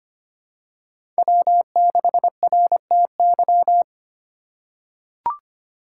Morse code, heard as a keyed tone, about 25 wpm, 700 Hz. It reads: W6RTY
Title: USA Call Signs - 5 Characters - 4x Character Spacing - 25wpm